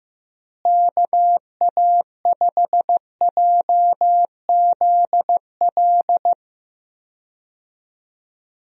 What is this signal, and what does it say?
Morse code, heard as a keyed tone, about 15 wpm, 700 Hz. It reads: KA5JZL
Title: USA Call Signs - 6 Characters - Repeated 2x - 15wpm